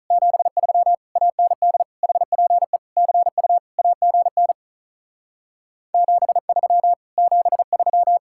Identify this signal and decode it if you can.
Morse code, heard as a keyed tone, about 43 wpm, 700 Hz. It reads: 73 AND HPE CU AGN 73 73
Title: Speed-Racing Common QSO Phrases - 35wpm